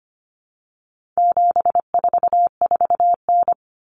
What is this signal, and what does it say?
Morse code, heard as a keyed tone, about 25 wpm, 700 Hz. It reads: 744N
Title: Common Cut Numbers - 4-Digit 25wpm